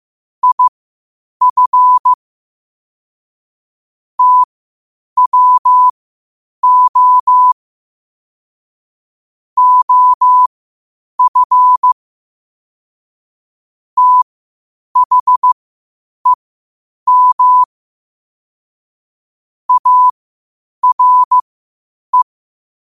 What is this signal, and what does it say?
Morse code, heard as a keyed tone, about 15 wpm, spaced as for 8 wpm, 1000 Hz. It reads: IF TWO OF THEM ARE
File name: qotdfiles / 8wpm